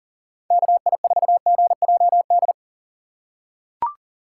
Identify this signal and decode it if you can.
Morse code, heard as a keyed tone, about 39 wpm, 700 Hz. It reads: KI4GJD
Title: USA Call Signs - 6 Characters - 3x Character Spacing - 40wpm